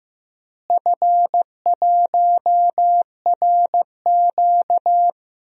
Morse code, heard as a keyed tone, about 15 wpm, 700 Hz. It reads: F1RQ
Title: DXCC-100 Call Signs - 4 Characters - Repeated 2x - 15wpm